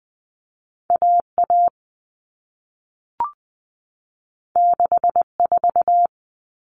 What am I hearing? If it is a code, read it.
Morse code, heard as a keyed tone, about 20 wpm, 700 Hz. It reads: A A 6 4
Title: All Cut Numbers - 2-Digit 20wpm